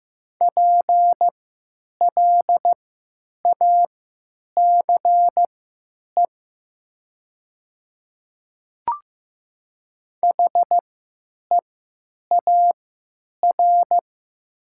Morse code, heard as a keyed tone, about 15 wpm, spaced as for 9 wpm, 700 Hz. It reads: PLACE HEAR
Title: 3x Character Spacing - Top 500 Words - 15wpm